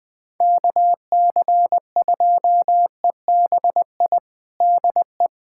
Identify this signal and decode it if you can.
Morse code, heard as a keyed tone, about 20 wpm, 700 Hz. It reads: KC2EBI DE
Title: USA Callsign DE Callsign 20wpm